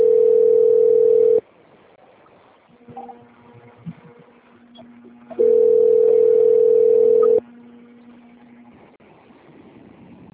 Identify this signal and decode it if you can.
Ringback tone (North America), 440 + 480 Hz.
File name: rg-1600-unknown- - - .4264